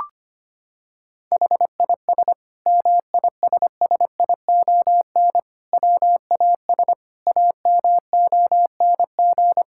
Morse code, heard as a keyed tone, about 25 wpm, 700 Hz. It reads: HIS MISSION WAS AMONG
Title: Sentences from Top 1600 Words 25wpm